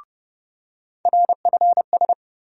Morse code, heard as a keyed tone, about 30 wpm, 700 Hz. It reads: RFS
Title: USA Call Sign Postfixes - 3 Letters 30wpm